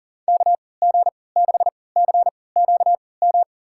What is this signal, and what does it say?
Morse code, heard as a keyed tone, about 40 wpm, spaced as for 23 wpm, 700 Hz. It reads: KG6CQM
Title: USA Call Signs - 6 Characters - 3x Character Spacing - 40wpm